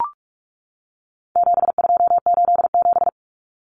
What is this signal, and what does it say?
Morse code, heard as a keyed tone, about 44 wpm, 700 Hz. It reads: 7287
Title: Common Cut Numbers - 4-Digit 45wpm